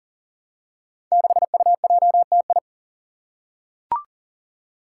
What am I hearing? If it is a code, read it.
Morse code, heard as a keyed tone, about 39 wpm, 700 Hz. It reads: 6UJTI
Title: Five Challenging Letter-Number Combinations 40wpm